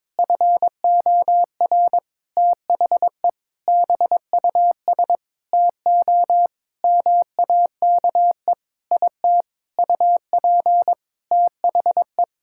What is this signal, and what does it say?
Morse code, heard as a keyed tone, about 22 wpm, 700 Hz. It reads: FOR THE BUS TO MAKE IT UP THE
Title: Sentences from Top 1000 Words - Review Part 6 - 22wpm